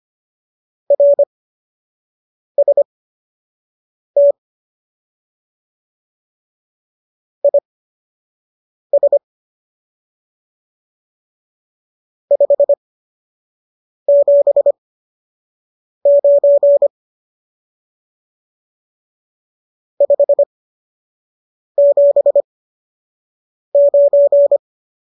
Morse code, heard as a keyed tone, about 25 wpm, spaced as for 6 wpm, 570 Hz. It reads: RST IS 579 579